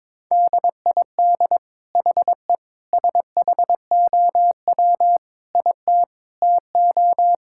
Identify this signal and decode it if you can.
Morse code, heard as a keyed tone, about 22 wpm, 700 Hz. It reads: DID HE SHOW IT TO
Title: Sentences from Top 1600 Words 22wpm